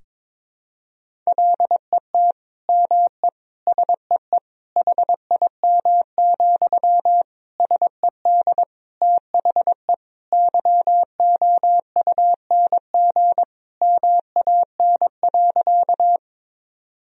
Morse code, heard as a keyed tone, about 22 wpm, 700 Hz. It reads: LET ME SEE HIM, SED THE YOUNG MAN.
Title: Sentences from Top 400 Words and Abbreviations 22wpm